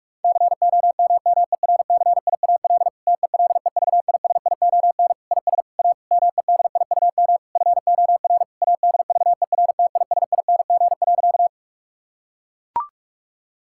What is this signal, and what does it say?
Morse code, heard as a keyed tone, about 41 wpm, 700 Hz. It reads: COMMERCIAL TELEVISION IS A MEDIUM FOR ADVERTISING.
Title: Sentences from Top 1000 Words - Review Part 6 - 45wpm